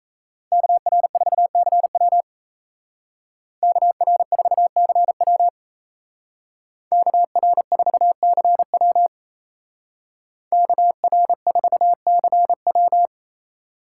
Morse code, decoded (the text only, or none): KR4CW KR4CW KR4CW KR4CW